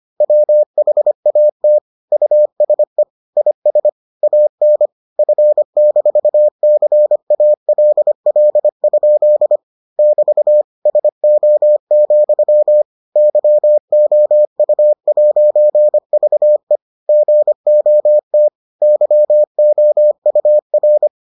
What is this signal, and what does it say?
Morse code, heard as a keyed tone, about 25 wpm, 600 Hz. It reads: WHAT USE IS AN F-CALL? <BT> SO, YOU'VE GOT YOUR